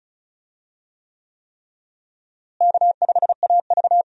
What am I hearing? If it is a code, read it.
Morse code, heard as a keyed tone, about 36 wpm, 700 Hz. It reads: K5AV
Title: USA Call Signs - 4 Characters - Repeated 3x - 35wpm